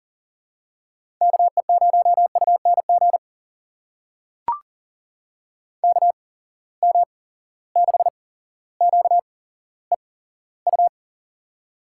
Morse code, heard as a keyed tone, about 40 wpm, spaced as for 17 wpm, 700 Hz. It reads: KE0UNG KM6QEU
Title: USA Call Signs - 6 Characters - 8x Character Spacing - 40wpm